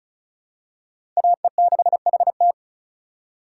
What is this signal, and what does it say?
Morse code, heard as a keyed tone, about 34 wpm, 700 Hz. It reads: AE6HT
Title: USA Call Signs - 5 Characters - 4x Character Spacing - 35wpm